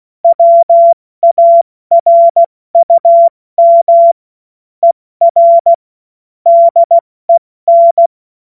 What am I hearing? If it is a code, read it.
Morse code, heard as a keyed tone, about 16 wpm, 680 Hz. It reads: WARUM ER DEN
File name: Morsekurs / Witze / Witz 05. April 2022 70 F80